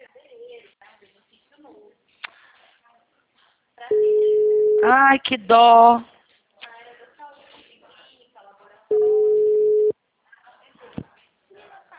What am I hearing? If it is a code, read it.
Ringback tone (Europe, 425 Hz), with speech.
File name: out- -2010- - - .65873